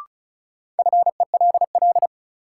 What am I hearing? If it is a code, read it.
Morse code, heard as a keyed tone, about 34 wpm, 700 Hz. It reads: FELL
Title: Sentences from Top 1600 Words - New Word Drill - 35wpm